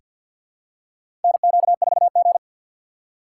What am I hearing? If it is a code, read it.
Morse code, heard as a keyed tone, about 47 wpm, 690 Hz. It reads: NQ4G
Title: USA Call Signs - 2x Character Spacing - 50wpm